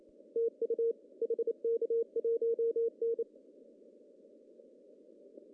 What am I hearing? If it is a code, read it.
Morse code, heard as a keyed tone, about 28 wpm, 460 Hz. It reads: TU HK1N
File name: CW / cqww-cw-k5tr